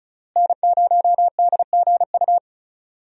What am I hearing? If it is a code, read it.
Morse code, heard as a keyed tone, about 34 wpm, 700 Hz. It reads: N0DGU
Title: USA Call Signs - 5 Characters - Repeated 3x - 35wpm